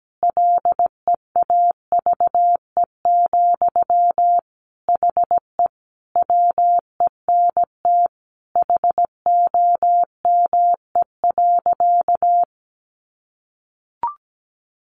Morse code, heard as a keyed tone, about 17 wpm, 700 Hz. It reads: LEAVE, HE WENT HOME.
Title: Sentences from Top 1200 Words 17wpm